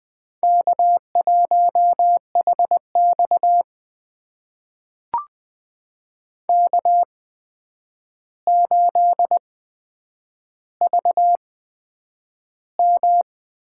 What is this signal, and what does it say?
Morse code, heard as a keyed tone, about 20 wpm, spaced as for 9 wpm, 700 Hz. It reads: K1HX K8VM
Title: USA Call Signs - 4 Characters - 8x Character Spacing - 20wpm